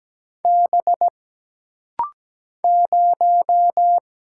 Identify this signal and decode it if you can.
Morse code, heard as a keyed tone, about 17 wpm, 700 Hz. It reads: B 0
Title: Single Letter-Number - Rapid-Fire 17wpm